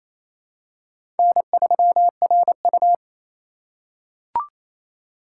Morse code, heard as a keyed tone, about 28 wpm, 700 Hz. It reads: N3RU
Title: USA Call Signs - 4 Characters - 2x Character Spacing - 28wpm